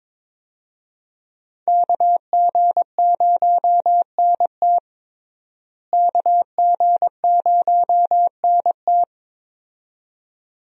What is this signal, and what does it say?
Morse code, heard as a keyed tone, about 22 wpm, 700 Hz. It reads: KG0NT KG0NT
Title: USA Call Signs - 5 Characters - Repeated 2x - 22wpm